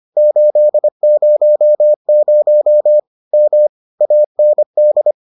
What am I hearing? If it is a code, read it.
Morse code, heard as a keyed tone, about 25 wpm, 600 Hz. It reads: 800 M AND